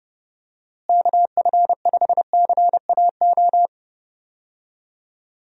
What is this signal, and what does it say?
Morse code, heard as a keyed tone, about 32 wpm, 700 Hz. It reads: KF5CAO
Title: USA Call Signs - 6 Characters - Repeated 3x - 30wpm